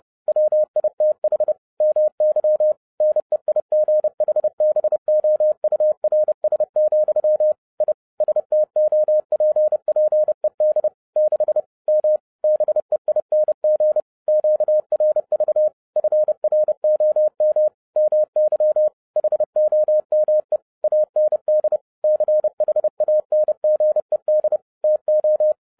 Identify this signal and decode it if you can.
Morse code, heard as a keyed tone, about 30 wpm, 610 Hz. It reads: WITH MY NEIGHBOURS, I STOPPED 6 M BEING QRV FROM MY HOME AND CHANGED TO